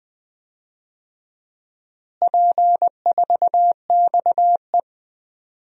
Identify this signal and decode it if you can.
Morse code, heard as a keyed tone, about 20 wpm, 700 Hz. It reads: P4XE
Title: DXCC-100 Call Signs - 4 Characters - Repeated 3x - 20wpm